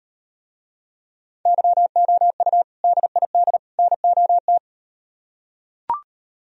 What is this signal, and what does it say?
Morse code, heard as a keyed tone, about 38 wpm, 700 Hz. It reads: YOU DID NOT